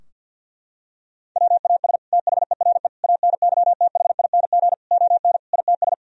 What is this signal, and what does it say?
Morse code, heard as a keyed tone, about 50 wpm, 700 Hz. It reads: WAS THERE ANYTHING ON ITS
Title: Sentences from Top 1000 Words - Review Part 3 - 50wpm